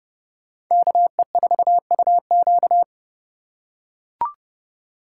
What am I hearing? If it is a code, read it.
Morse code, heard as a keyed tone, about 29 wpm, 700 Hz. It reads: KE4UQ